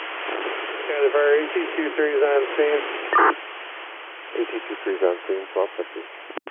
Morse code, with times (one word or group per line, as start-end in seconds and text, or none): none